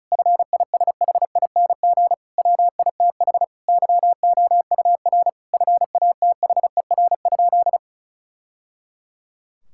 Morse code, heard as a keyed tone, about 36 wpm, 700 Hz. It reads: FISHING WITH YOUR FATHER?
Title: Sentences from Top 1000 Words - Review Part 5 - 35wpm